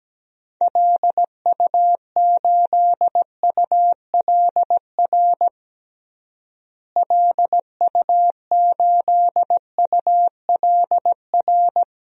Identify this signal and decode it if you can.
Morse code, heard as a keyed tone, about 17 wpm, 700 Hz. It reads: LU8ULR LU8ULR
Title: DXCC-100 Call Signs - 6 Characters - Repeated 3x - 17wpm